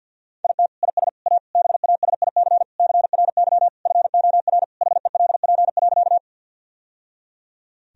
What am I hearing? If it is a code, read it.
Morse code, heard as a keyed tone, about 50 wpm, 700 Hz. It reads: IT IS A BASIC CRY FOR HELP.
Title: Sentences from Top 1000 Words - Review Part 3 - 50wpm